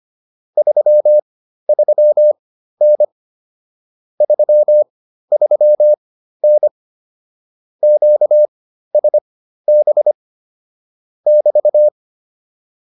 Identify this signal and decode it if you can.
Morse code, heard as a keyed tone, about 25 wpm, spaced as for 13 wpm, 600 Hz. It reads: 33N 33N QSB <BT>